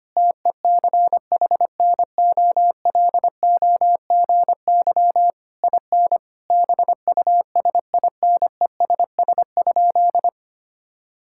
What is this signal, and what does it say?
Morse code, heard as a keyed tone, about 25 wpm, 700 Hz. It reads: TECHNOLOGY IN BUSINESS?
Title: Sentences from Top 1000 Words - Review Part 5 - 25wpm